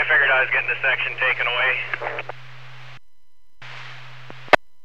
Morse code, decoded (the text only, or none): none